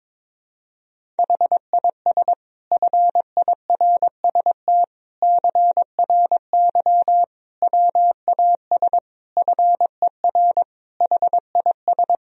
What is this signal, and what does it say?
Morse code, heard as a keyed tone, about 22 wpm, 700 Hz. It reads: HIS FIRST CRY WAS FER HIS
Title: Sentences from Top 500 Words and Abbreviations 22wpm